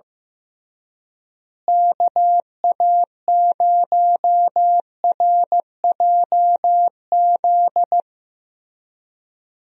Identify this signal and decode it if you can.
Morse code, heard as a keyed tone, about 16 wpm, 700 Hz. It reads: KA0RJZ
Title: USA Call Signs - 6 Characters - Repeated 3x - 15wpm